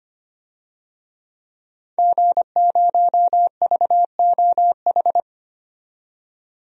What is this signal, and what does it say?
Morse code, heard as a keyed tone, about 25 wpm, 700 Hz. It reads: G0VOH
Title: DXCC-100 Call Signs - 5 Characters - 25wpm